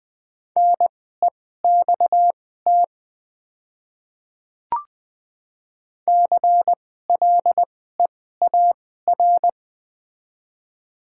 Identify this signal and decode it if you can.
Morse code, heard as a keyed tone, about 20 wpm, spaced as for 14 wpm, 690 Hz. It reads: NEXT CLEAR
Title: 2x Character Spacing - Top 500 Words - 20wpm